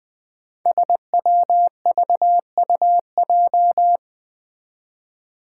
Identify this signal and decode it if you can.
Morse code, heard as a keyed tone, about 20 wpm, 700 Hz. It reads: SWVUJ
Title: Five Challenging Letter-Number Combinations 20wpm